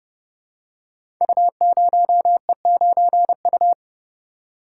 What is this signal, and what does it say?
Morse code, heard as a keyed tone, about 31 wpm, 700 Hz. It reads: U0E9U
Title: Code Groups - 5 Letter:Numbers - 30wpm